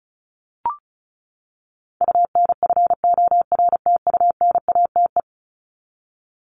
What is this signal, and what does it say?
Morse code, heard as a keyed tone, about 34 wpm, 700 Hz. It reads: UNFORTUNATE
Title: LY Suffix - Top 100 Words with Base Words 35wpm